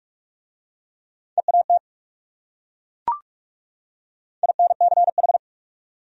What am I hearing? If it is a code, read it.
Morse code, heard as a keyed tone, about 37 wpm, 700 Hz. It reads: EAT INCH
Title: Speed-Racing Top 500 Words 30wpm